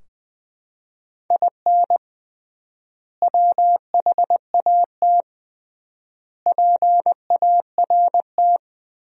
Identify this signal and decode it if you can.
Morse code, heard as a keyed tone, about 20 wpm, 700 Hz. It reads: IN WHAT PART
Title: 3x Word-Spacing - Sentences from Top 200 Words - 20wpm